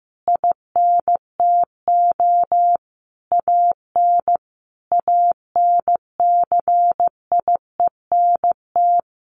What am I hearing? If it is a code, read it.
Morse code, heard as a keyed tone, about 15 wpm, 700 Hz. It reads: INTO AN ANCIENT
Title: A Sound Of Thunder 15wpm